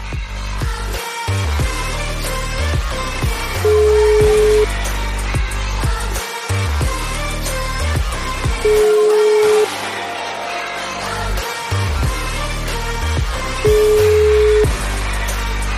Ringback tone (Europe, 425 Hz), with noise.